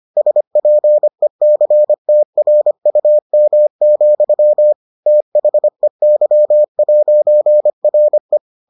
Morse code, heard as a keyed tone, about 25 wpm, 600 Hz. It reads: SPECTRUM, THEY'RE